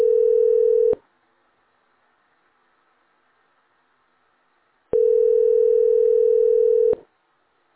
Ringback tone (North America), 440 + 480 Hz.
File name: 9497e41a-a203-413f-97ba